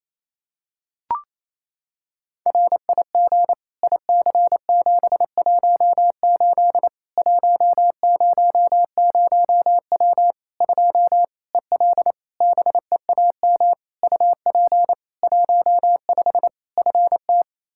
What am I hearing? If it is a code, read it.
Morse code, heard as a keyed tone, about 27 wpm, 700 Hz. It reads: RIG IC718 100W 2 EL BEAM UP 15 FT
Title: Practice QSO - Exchange 2 - Info - 28wpm